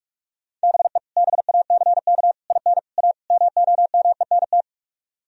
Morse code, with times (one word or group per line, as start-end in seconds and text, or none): 0.6-1.0 BE
1.2-2.3 BACK
2.5-2.8 IN
3.0-3.1 A
3.3-4.6 MOMENT